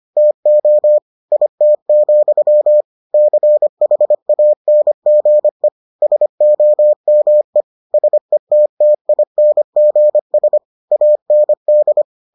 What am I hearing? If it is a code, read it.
Morse code, heard as a keyed tone, about 25 wpm, 600 Hz. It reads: TO IT, CHANGE SOME SETTINGS AND